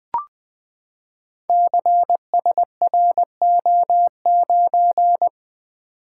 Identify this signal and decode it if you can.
Morse code, heard as a keyed tone, about 20 wpm, 700 Hz. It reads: CSRO9
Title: Code Groups - 5 Letter:Numbers - 20wpm